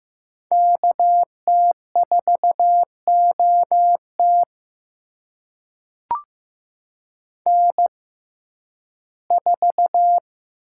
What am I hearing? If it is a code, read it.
Morse code, heard as a keyed tone, about 15 wpm, spaced as for 11 wpm, 700 Hz. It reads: KT4OT N4
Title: USA Call Signs - 5 Characters - 6x Character Spacing - 15wpm